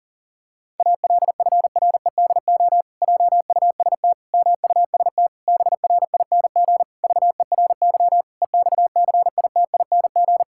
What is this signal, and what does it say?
Morse code, heard as a keyed tone, about 40 wpm, 700 Hz. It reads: ALFREDO JUST MUST BRING VERY EXCITING
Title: Pangram Challenge v2 - 40wpm